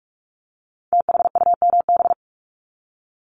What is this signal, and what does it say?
Morse code, heard as a keyed tone, about 46 wpm, 700 Hz. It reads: THUMB
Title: Multi-letter Phonemes 45wpm